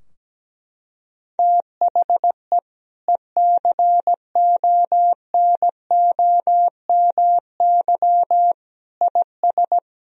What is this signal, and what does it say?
Morse code, heard as a keyed tone, about 17 wpm, 700 Hz. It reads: THE ECONOMY IS
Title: Sentences from Top 1000 Words - Review Part 3 - 17wpm